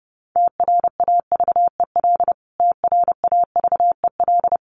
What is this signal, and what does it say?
Morse code, heard as a keyed tone, about 30 wpm, 700 Hz. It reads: TRAVEL TRAVEL
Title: Top 500 Words Repeated 30wpm